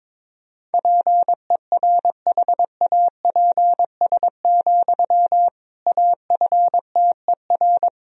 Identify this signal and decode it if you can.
Morse code, heard as a keyed tone, about 22 wpm, 700 Hz. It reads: PERHAPS, AFTER
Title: Sentences from Top 500 Words and Abbreviations 22wpm